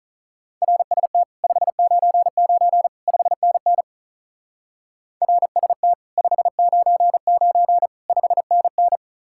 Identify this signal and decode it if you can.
Morse code, heard as a keyed tone, about 46 wpm, 700 Hz. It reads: RST 599 5NN RST 599 5NN